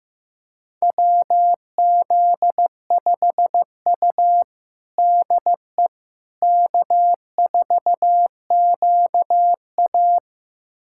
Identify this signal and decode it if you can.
Morse code, heard as a keyed tone, about 15 wpm, 700 Hz. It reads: WZ5U DE K4QA
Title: USA Callsign DE Callsign - 4 Characters - 15wpm